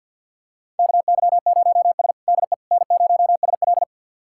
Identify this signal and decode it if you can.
Morse code, heard as a keyed tone, about 46 wpm, 700 Hz. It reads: KY0S DE N0SL
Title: USA Callsign DE Callsign - 4 Characters - 50wpm